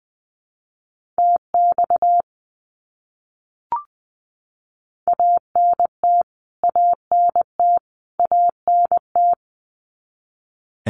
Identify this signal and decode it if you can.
Morse code, heard as a keyed tone, about 20 wpm, 700 Hz. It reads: TX ANT ANT ANT
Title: Abbreviations Repeated 3x - 20wpm